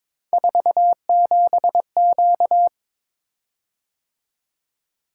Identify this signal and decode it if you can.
Morse code, heard as a keyed tone, about 23 wpm, 700 Hz. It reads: 47Q
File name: Three Letter-Number Combinations - 22wpm